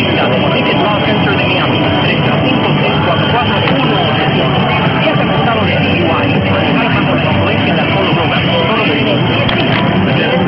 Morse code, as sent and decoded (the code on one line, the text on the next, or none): none